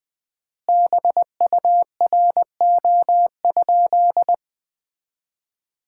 Morse code, heard as a keyed tone, about 20 wpm, 700 Hz. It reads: BURO?
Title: Common QSO Phrases - 4x Word-Spacing - 20wpm